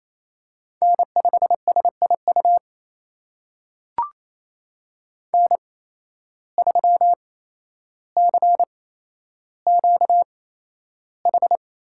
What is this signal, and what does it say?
Morse code, heard as a keyed tone, about 28 wpm, spaced as for 12 wpm, 700 Hz. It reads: N5SIU N3CQH
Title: USA Call Signs - 5 Characters - 8x Character Spacing - 28wpm